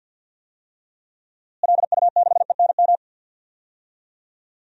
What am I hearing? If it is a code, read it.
Morse code, heard as a keyed tone, about 51 wpm, 700 Hz. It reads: LU6ENM
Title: DXCC-100 Call Signs - 6 Characters - Repeated 3x - 50wpm